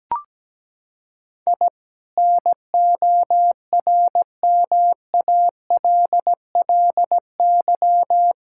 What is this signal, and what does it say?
Morse code, heard as a keyed tone, about 17 wpm, 700 Hz. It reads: I NORMALLY
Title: Sentences from Top 1000 Words - Review Part 6 - 17wpm